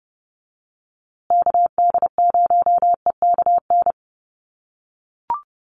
Morse code, heard as a keyed tone, about 30 wpm, 700 Hz. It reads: KD0EKN
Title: USA Call Signs Repeated 3x 30wpm